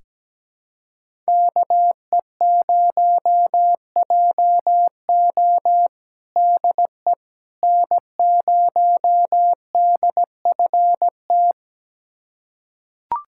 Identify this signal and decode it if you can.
Morse code, heard as a keyed tone, about 17 wpm, 700 Hz. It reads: KE0JO DE N0DFT
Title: USA Callsign DE Callsign - 5 Characters - 17wpm